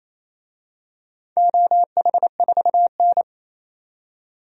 Morse code, heard as a keyed tone, about 28 wpm, 700 Hz. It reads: OH4N
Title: DXCC-100 Call Signs - 4 Characters - 28wpm